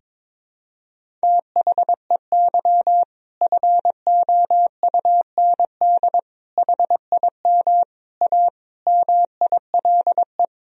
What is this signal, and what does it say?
Morse code, heard as a keyed tone, about 22 wpm, 700 Hz. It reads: THEY FOUND HIM A MILE
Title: Sentences from Top 1000 Words - Review Part 1 - 22wpm